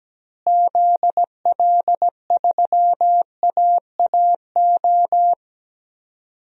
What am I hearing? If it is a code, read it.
Morse code, heard as a keyed tone, about 17 wpm, 700 Hz. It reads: ZL3AAO
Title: DXCC-100 Call Signs - 6 Characters - Repeated 2x - 17wpm